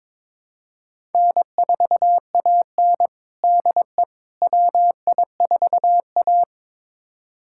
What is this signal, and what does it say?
Morse code, heard as a keyed tone, about 22 wpm, 700 Hz. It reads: N4AN DE WI4A
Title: USA Callsign DE Callsign - 4 Characters - 22wpm